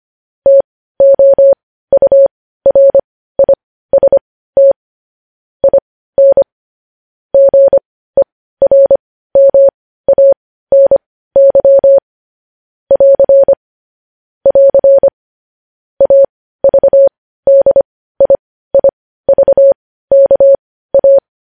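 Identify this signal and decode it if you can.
Morse code, heard as a keyed tone, about 25 wpm, spaced as for 15 wpm, 550 Hz. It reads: TOURIST IN GERMANY <AR> <AR> AVDIIVKA
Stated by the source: Morse code news headlines sent at 15, opening with vvv.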